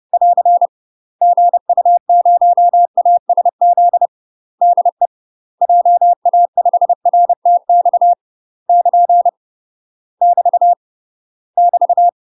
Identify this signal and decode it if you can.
Morse code, heard as a keyed tone, about 30 wpm, 700 Hz. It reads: <AR> GU0ASZ DE JA5RTX <KN> <BT> <BT>
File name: UK MASTER QSOs TRAINING / 30WPM / UK MASTER QSOs 0005